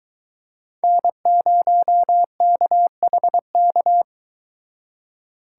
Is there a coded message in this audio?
Yes: Morse code from 0.8 to 4.0 s, heard as a keyed tone, about 24 wpm, 700 Hz. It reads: N0KHK